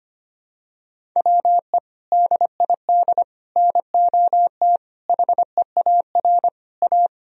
Morse code, heard as a keyed tone, about 25 wpm, 700 Hz. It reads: WE DID NOT HEAR A
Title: Sentences from Top 1000 Words - Review Part 4 - 25wpm